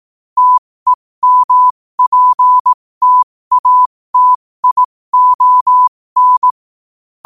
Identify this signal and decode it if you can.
Morse code, heard as a keyed tone, about 18 wpm, spaced as for 15 wpm, 1000 Hz. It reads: TEMPTATION
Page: 15 wpm morse code quotes